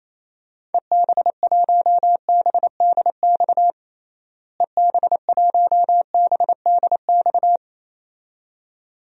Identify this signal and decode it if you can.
Morse code, heard as a keyed tone, about 28 wpm, 700 Hz. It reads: EB1BDX EB1BDX
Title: DXCC-100 Call Signs - 6 Characters - Repeated 2x - 28wpm